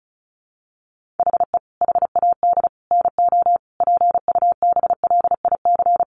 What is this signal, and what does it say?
Morse code, heard as a keyed tone, about 34 wpm, 700 Hz. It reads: HE HAD NO PUBLIC
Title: Sentences from Top 1500 Words 35wpm